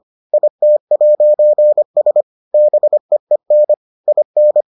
Morse code, heard as a keyed tone, about 25 wpm, 600 Hz. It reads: IT'S BEEN IN